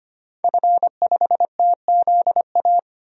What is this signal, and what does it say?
Morse code, heard as a keyed tone, about 25 wpm, 700 Hz. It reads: F5TZA
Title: DXCC-100 Call Signs - 5 Characters - 25wpm